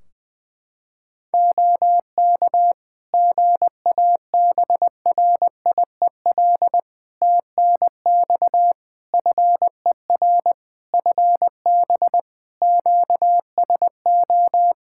Morse code, heard as a keyed tone, about 21 wpm, 700 Hz. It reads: OK GABRIEL TNX FER FB QSO
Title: Practice QSO - Exchange 4 - Ending - 20wpm